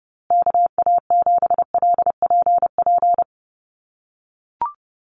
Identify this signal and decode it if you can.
Morse code, heard as a keyed tone, about 30 wpm, 700 Hz. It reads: KA7LPP
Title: USA Call Signs Repeated 2x 30wpm